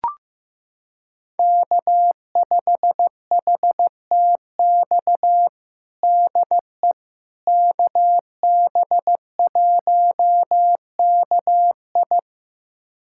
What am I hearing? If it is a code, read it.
Morse code, heard as a keyed tone, about 16 wpm, 700 Hz. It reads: K5HTX DE KB1KI
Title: USA Callsign DE Callsign - 5 Characters - 15wpm